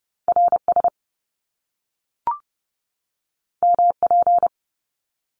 Morse code, heard as a keyed tone, about 30 wpm, 700 Hz. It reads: RS MP
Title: Top Two Letter Combinations in Words 30wpm